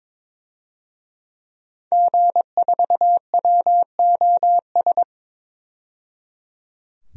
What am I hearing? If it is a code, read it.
Morse code, heard as a keyed tone, about 22 wpm, 700 Hz. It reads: G4WOS
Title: DXCC-100 Call Signs - 5 Characters - 22wpm